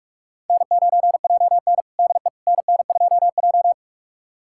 Morse code, heard as a keyed tone, about 44 wpm, 700 Hz. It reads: N9JN DE NN2J
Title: USA Callsign DE Callsign - 4 Characters - 45wpm